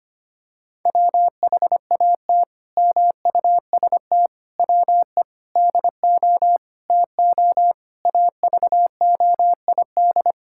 Morse code, heard as a keyed tone, about 25 wpm, 700 Hz. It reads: WHAT MUST WE DO TO AVOID
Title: Sentences from Top 1000 Words - Review Part 6 - 25wpm